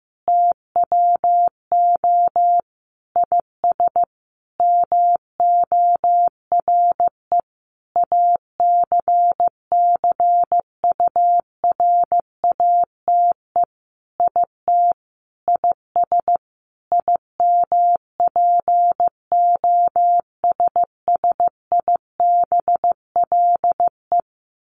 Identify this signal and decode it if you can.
Morse code, heard as a keyed tone, about 15 wpm, 700 Hz. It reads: TWO IS MORE ACCURATE IT IS IMPOSSIBLE
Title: Sentences from Top 1200 Words 15wpm